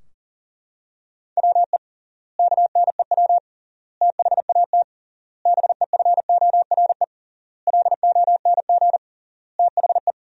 Morse code, heard as a keyed tone, about 40 wpm, 700 Hz. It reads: WE KNEW THAT BEFORE LONG THE
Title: 3x Word-Spacing - Sentences from Top 400 Words - 40wpm